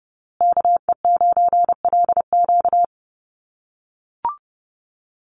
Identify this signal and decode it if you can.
Morse code, heard as a keyed tone, about 30 wpm, 700 Hz. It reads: KE9LQ
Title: USA Call Signs Repeated 3x 30wpm